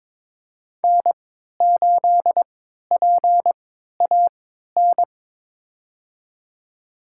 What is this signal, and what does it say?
Morse code, heard as a keyed tone, about 22 wpm, spaced as for 13 wpm, 700 Hz. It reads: N8PAN
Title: USA Call Signs - 3x Character Spacing - 22wpm